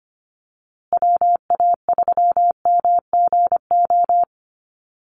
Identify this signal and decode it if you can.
Morse code, heard as a keyed tone, about 25 wpm, 700 Hz. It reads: WA3MGO
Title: USA Call Signs Repeated 3x 25wpm